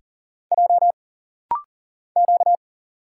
Morse code, heard as a keyed tone, about 40 wpm, 700 Hz. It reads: J Q
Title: Single Letters - Rapid-Fire 40wpm